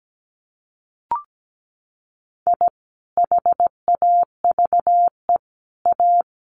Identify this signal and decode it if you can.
Morse code, heard as a keyed tone, about 17 wpm, 700 Hz. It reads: I HAVE A
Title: Sets of 3 Words 17wpm